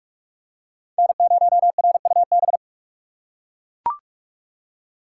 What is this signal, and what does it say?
Morse code, heard as a keyed tone, about 41 wpm, 700 Hz. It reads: N0RUB